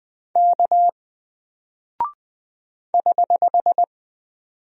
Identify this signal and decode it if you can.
Morse code, heard as a keyed tone, about 20 wpm, 700 Hz. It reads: K <HH>
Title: Single Character (Letter:Number:Punctuation:Prosign) Alt 20wpm